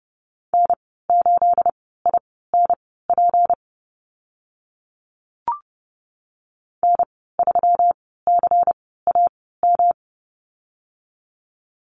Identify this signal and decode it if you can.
Morse code, heard as a keyed tone, about 30 wpm, spaced as for 17 wpm, 700 Hz. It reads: N8INP N3CAM
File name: Lesson 37g - Letter x - Call Signs 30wpm